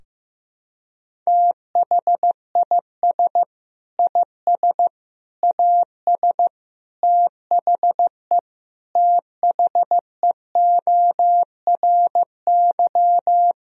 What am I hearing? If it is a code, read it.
Morse code, heard as a keyed tone, about 15 wpm, 700 Hz. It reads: THIS IS AS THE THEORY
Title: Sentences from Top 1000 Words - Review Part 3 - 15wpm